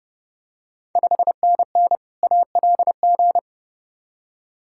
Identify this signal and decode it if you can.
Morse code, heard as a keyed tone, about 30 wpm, 700 Hz. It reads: 5NN ALG
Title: Ontario QSO Party Exchange 30wpm